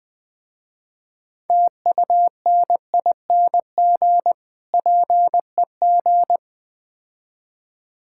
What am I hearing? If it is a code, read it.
Morse code, heard as a keyed tone, about 21 wpm, 700 Hz. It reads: TUNING PEG
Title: Difficult Words - Speed-Racing with Hints - 20wpm